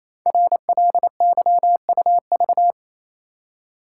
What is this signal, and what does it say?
Morse code, heard as a keyed tone, about 28 wpm, 700 Hz. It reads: RLYUV
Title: Code Groups - 5 Letters - 28wpm